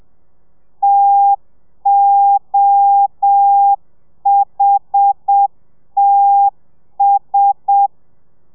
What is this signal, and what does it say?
Morse code, heard as a keyed tone, about 7 wpm, 800 Hz. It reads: TOHTS
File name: telegrafia / i2tfj / Lez1 30